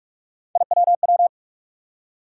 Morse code, heard as a keyed tone, about 46 wpm, 700 Hz. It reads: IWW
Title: USA Call Sign Postfixes - 3 Letters 45wpm